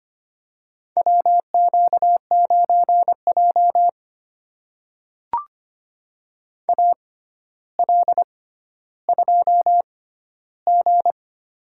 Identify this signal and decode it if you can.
Morse code, heard as a keyed tone, about 25 wpm, spaced as for 13 wpm, 700 Hz. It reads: WQ9J AL2G
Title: USA Call Signs - 4 Characters - 6x Character Spacing - 25wpm